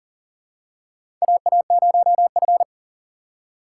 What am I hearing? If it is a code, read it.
Morse code, heard as a keyed tone, about 40 wpm, 700 Hz. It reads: AA0F
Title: USA Call Signs - 4 Characters - 4x Character Spacing - 40wpm